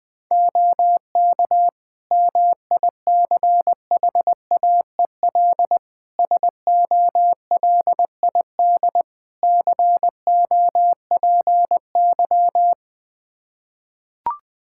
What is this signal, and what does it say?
Morse code, heard as a keyed tone, about 20 wpm, 700 Hz. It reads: OK MICHAEL SOLID COPY
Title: Practice QSO - Exchange 3 - More Info - 20wpm